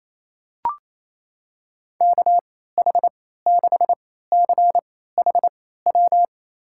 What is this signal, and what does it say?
Morse code, heard as a keyed tone, about 28 wpm, spaced as for 16 wpm, 700 Hz. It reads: KH6CHW
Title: USA Call Signs - 6 Characters - 3x Character Spacing - 28wpm